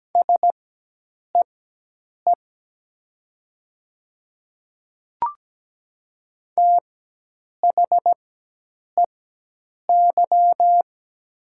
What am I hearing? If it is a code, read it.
Morse code, heard as a keyed tone, about 17 wpm, spaced as for 8 wpm, 700 Hz. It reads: SEE THEY
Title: 4x Character Spacing - Top 200 Words - 17wpm